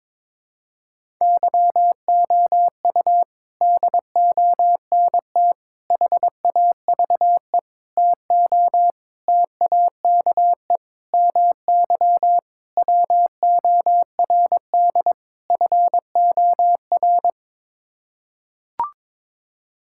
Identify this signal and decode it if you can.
Morse code, heard as a keyed tone, about 22 wpm, 700 Hz. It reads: YOU DONT HAVE TO TAKE MY WORD FOR
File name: practice / 22wpm / Sets of 8 Words - 22wpm